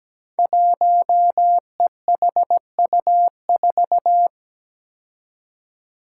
Morse code, heard as a keyed tone, about 17 wpm, 700 Hz. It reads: 1EHU4
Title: Five Challenging Letter-Number Combinations 17wpm